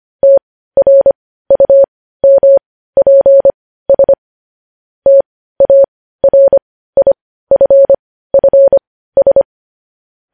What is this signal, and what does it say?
Morse code, heard as a keyed tone, about 25 wpm, spaced as for 15 wpm, 550 Hz. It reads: TRUMPS TARIFFS
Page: news-headlines-in-morse-code-at-15-wpm